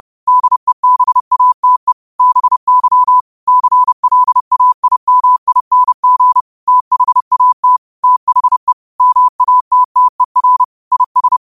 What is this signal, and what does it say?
Morse code, heard as a keyed tone, about 30 wpm, 1000 Hz. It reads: DEBATE BY CLAIMING THAT THE MATTER IS
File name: qotdfiles / 30wpm